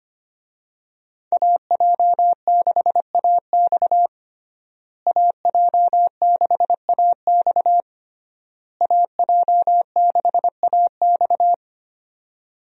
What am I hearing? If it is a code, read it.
Morse code, heard as a keyed tone, about 25 wpm, 700 Hz. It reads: AJ6AX AJ6AX AJ6AX
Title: USA Call Signs - 5 Characters - Repeated 3x - 25wpm